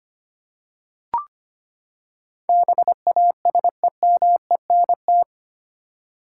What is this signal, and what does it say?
Morse code, heard as a keyed tone, about 25 wpm, 700 Hz. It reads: BASEMENT
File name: Eight Letter Words - 25wpm